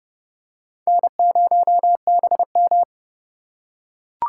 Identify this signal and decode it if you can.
Morse code, heard as a keyed tone, about 30 wpm, 700 Hz. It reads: N0BM
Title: USA Call Signs - 4 Characters - 3x Character Spacing - 30wpm